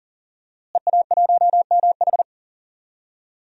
Morse code, heard as a keyed tone, about 38 wpm, 700 Hz. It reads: EA1MH
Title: DXCC-100 Call Signs - 5 Characters - Repeated 3x - 40wpm